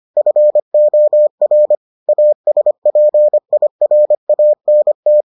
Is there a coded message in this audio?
Yes: Morse code from 0.2 to 5.2 s, heard as a keyed tone, about 25 wpm, 600 Hz. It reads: FOR ASPIRANT